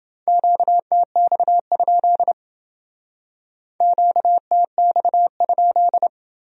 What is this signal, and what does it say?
Morse code, heard as a keyed tone, about 29 wpm, 700 Hz. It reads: QTX? QTX?